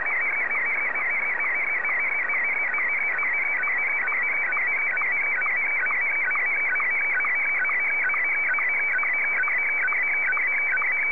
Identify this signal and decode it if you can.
SSTV (slow-scan television), Martin 1.